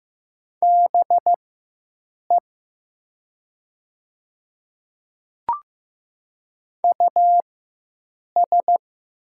Morse code, heard as a keyed tone, about 15 wpm, spaced as for 7 wpm, 700 Hz. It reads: BE US
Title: 4x Character Spacing - Top 200 Words - 15wpm